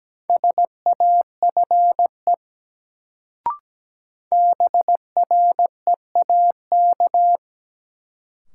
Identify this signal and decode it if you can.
Morse code, heard as a keyed tone, about 17 wpm, 700 Hz. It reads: SAFE BREAK
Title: Sentences from Top 900 Words - New Word Drill - 17wpm